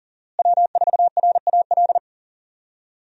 Morse code, heard as a keyed tone, about 40 wpm, 700 Hz. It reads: W4RAL
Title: USA Call Signs - 5 Characters - Repeated 2x - 40wpm